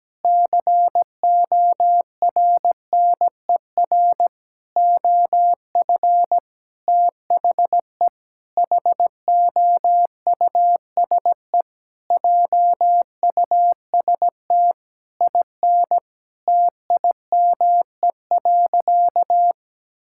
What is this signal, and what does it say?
Morse code, heard as a keyed tone, about 17 wpm, 700 Hz. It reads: CORNER OF THE HOUSE JUST IN TIME.
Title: Sentences from Top 1600 Words 17wpm